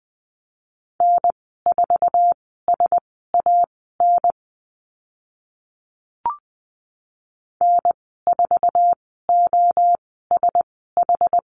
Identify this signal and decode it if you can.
Morse code, heard as a keyed tone, about 20 wpm, spaced as for 14 wpm, 700 Hz. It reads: N4SAN N4OSH
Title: 069 Lesson 12g - Letter d - Call Signs 20wpm